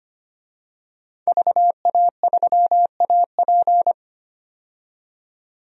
Morse code, heard as a keyed tone, about 25 wpm, 700 Hz. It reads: VA3AP
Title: DXCC-100 Call Signs - 5 Characters - 25wpm